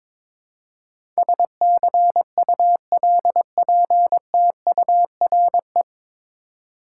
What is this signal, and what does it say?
Morse code, heard as a keyed tone, about 21 wpm, 700 Hz. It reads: SCULPTURE
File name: 9 Letter Words - First 3 Letters Given - 22wpm